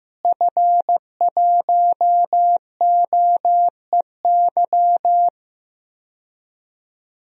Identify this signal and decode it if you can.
Morse code, heard as a keyed tone, about 15 wpm, 700 Hz. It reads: F1OEY
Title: DXCC-100 Call Signs - 5 Characters - Repeated 3x - 15wpm